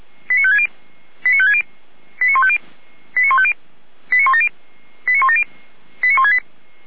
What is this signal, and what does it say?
Five-tone selective call (ZVEI), heard as quick runs of five tones, tones of about 70 ms. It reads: 89560 89560 89150 89150 89170 89170 89168